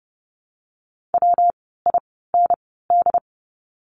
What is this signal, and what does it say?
Morse code, heard as a keyed tone, about 30 wpm, spaced as for 17 wpm, 700 Hz. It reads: WIND
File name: course / 30:10wpm / 115 Lesson 19f - Letter w - Words 30wpm